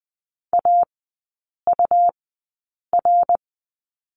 Morse code, heard as a keyed tone, about 20 wpm, spaced as for 8 wpm, 700 Hz. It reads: AUR
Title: 145 Lesson 24e - Letter y - Three Characters 20wpm